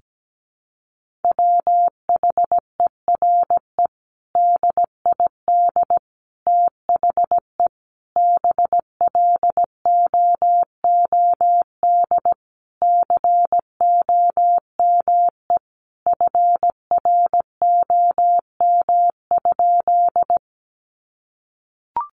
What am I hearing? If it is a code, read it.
Morse code, heard as a keyed tone, about 17 wpm, 700 Hz. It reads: WHERE DID THE BLOOD COME FROM?
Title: Sentences from Top 1100 Words 17wpm